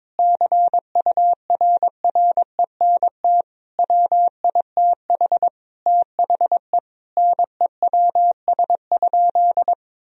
Morse code, heard as a keyed tone, about 22 wpm, 700 Hz. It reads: CURRENT WITH THE NEWS?
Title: Sentences from Top 1000 Words - Review Part 4 - 22wpm